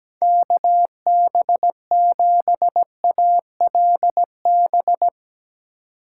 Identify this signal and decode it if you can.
Morse code, heard as a keyed tone, about 17 wpm, 700 Hz. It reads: KB7ALB
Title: USA Call Signs - 6x Character Spacing - 17wpm